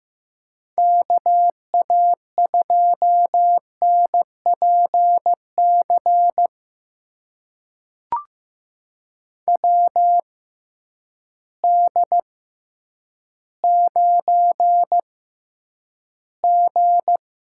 Morse code, heard as a keyed tone, about 15 wpm, spaced as for 9 wpm, 700 Hz. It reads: KA2NPC WD9G
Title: USA Call Signs - 6 Characters - 6x Character Spacing - 15wpm